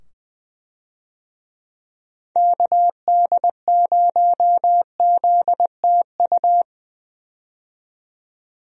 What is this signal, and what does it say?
Morse code, heard as a keyed tone, about 20 wpm, 700 Hz. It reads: KD0ZTU
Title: USA Call Signs - 6 Characters - 20wpm